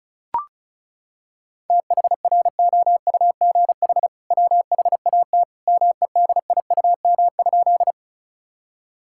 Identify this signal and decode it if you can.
Morse code, heard as a keyed tone, about 36 wpm, 700 Hz. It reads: THROUGH WHAT MEDIUM?
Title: Sentences from Top 1000 Words - Review Part 6 - 35wpm